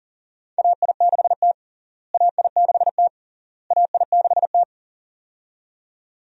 Morse code, heard as a keyed tone, about 40 wpm, 700 Hz. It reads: AI6T AI6T AI6T
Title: USA Call Signs - 4 Characters - Repeated 3x - 40wpm